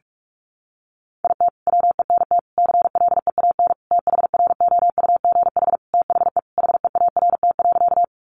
Morse code, heard as a keyed tone, about 44 wpm, 700 Hz. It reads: IT WENT CLEAN THROUGH THE HEART.
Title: Sentences from Top 1200 Words 45wpm